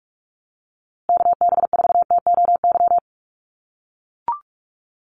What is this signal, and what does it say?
Morse code, heard as a keyed tone, about 44 wpm, 690 Hz. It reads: KB4TOY
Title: USA Call Signs Repeated 2x 45wpm